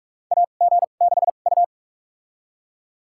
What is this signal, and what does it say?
Morse code, heard as a keyed tone, about 43 wpm, spaced as for 32 wpm, 700 Hz. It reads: AG6U
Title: USA Call Signs - 2x Character Spacing - 45wpm